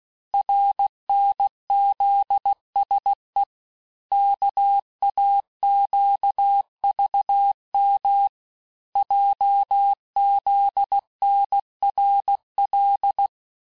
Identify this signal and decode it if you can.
Morse code, heard as a keyed tone, about 16 wpm, 800 Hz. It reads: RNZSE KAQVM JZNRL
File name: cours-du-lundi files / cours-1592--teste-chiffre-vitesse-15mpm